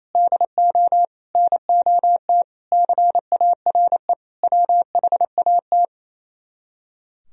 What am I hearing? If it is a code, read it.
Morse code, heard as a keyed tone, about 34 wpm, 700 Hz. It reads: DO NOT CARE WHAT
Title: Speed-Racing - Sets of 5 Words 28wpm